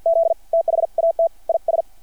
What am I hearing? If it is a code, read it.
Morse code, heard as a keyed tone, about 50 wpm, spaced as for 44 wpm, 660 Hz. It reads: Z THAT IS
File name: archived / CRSnet